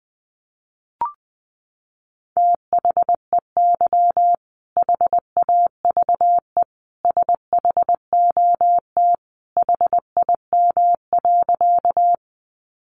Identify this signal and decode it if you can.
Morse code, heard as a keyed tone, about 19 wpm, 700 Hz. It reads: THEY HAVE SHOT HIM.
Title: Sentences from Top 1100 Words 20wpm